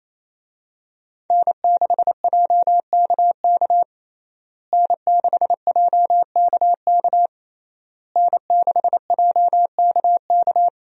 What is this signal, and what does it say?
Morse code, heard as a keyed tone, about 27 wpm, 700 Hz. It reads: N6JKK N6JKK N6JKK
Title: USA Call Signs - 5 Characters - Repeated 3x - 28wpm